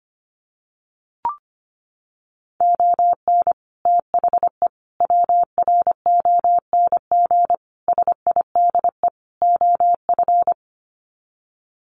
Morse code, heard as a keyed tone, about 25 wpm, 700 Hz. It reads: ON THE WRONG SIDE OF
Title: Sets of 5 Words Encore 25wpm